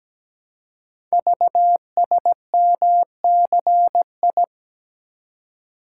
Morse code, heard as a keyed tone, about 17 wpm, 700 Hz. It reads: VSMCI
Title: Code Groups - 5 Letter:Numbers - 17wpm